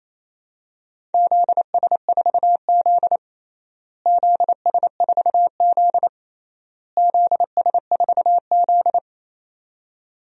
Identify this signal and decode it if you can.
Morse code, heard as a keyed tone, about 28 wpm, 700 Hz. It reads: ZS4Z ZS4Z ZS4Z
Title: DXCC-100 Call Signs - 4 Characters - Repeated 3x - 28wpm